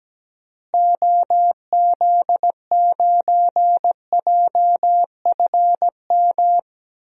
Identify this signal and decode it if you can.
Morse code, heard as a keyed tone, about 17 wpm, 700 Hz. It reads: OZ9JFM
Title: DXCC-100 Call Signs - 6 Characters - Repeated 2x - 17wpm